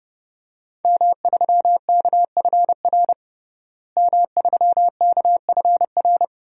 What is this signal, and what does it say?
Morse code, heard as a keyed tone, about 30 wpm, 700 Hz. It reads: M3KFR M3KFR
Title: DXCC-100 Call Signs - 5 Characters - Repeated 2x - 30wpm